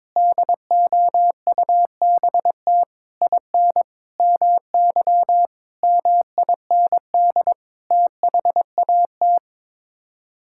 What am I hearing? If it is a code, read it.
Morse code, heard as a keyed tone, about 22 wpm, 700 Hz. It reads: DOUBT IN MY MIND THAT
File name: practice / 22wpm / Sets of 8 Words - 22wpm